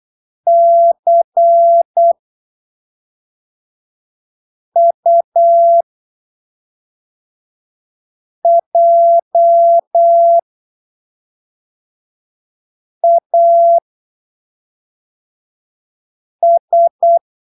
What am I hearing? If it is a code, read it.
Morse code, heard as a keyed tone, about 8 wpm, spaced as for 3 wpm, 680 Hz. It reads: CUJAS
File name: arquivos / texto / texto 01 08ppm